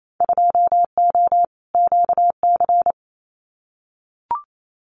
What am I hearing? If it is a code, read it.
Morse code, heard as a keyed tone, about 28 wpm, 700 Hz. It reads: 2O QC
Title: Winter Field Day Exchanges 28wpm